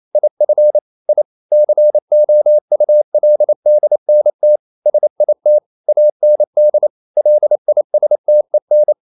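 Morse code, heard as a keyed tone, about 28 wpm, 600 Hz. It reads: IF I COULDNT SIT AND LISTEN
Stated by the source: Morse code practice at 28 wpm, text of Ken Follett's novel Never in English, from CW med Gnister.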